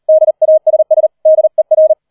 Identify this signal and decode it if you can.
Morse code, heard as a keyed tone, about 38 wpm, 630 Hz. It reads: DASS DER
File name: 40wpm